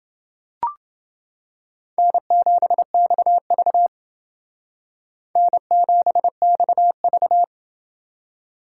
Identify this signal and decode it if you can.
Morse code, heard as a keyed tone, about 28 wpm, 700 Hz. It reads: N7XV N7XV